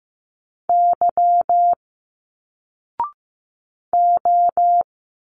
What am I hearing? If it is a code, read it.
Morse code, heard as a keyed tone, about 15 wpm, 700 Hz. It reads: Y O
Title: Single Letters 15wpm